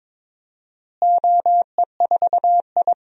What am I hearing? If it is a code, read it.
Morse code, heard as a keyed tone, about 22 wpm, 700 Hz. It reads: OE4I
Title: DXCC-100 Call Signs - 4 Characters - Repeated 3x - 22wpm